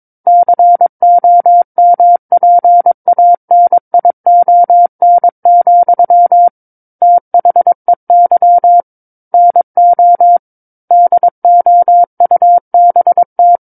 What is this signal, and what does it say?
Morse code, heard as a keyed tone, about 22 wpm, 700 Hz. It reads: COMPANION, THEY NO DOUBT
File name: Book One / 13-Chapter 13 - 22 WPM